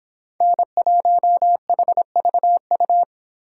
Morse code, heard as a keyed tone, about 26 wpm, 700 Hz. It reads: N1HVU